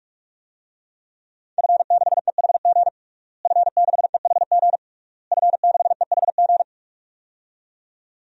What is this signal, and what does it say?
Morse code, heard as a keyed tone, about 46 wpm, 700 Hz. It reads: F6EHG F6EHG F6EHG
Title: DXCC-100 Call Signs - 5 Characters - Repeated 3x - 45wpm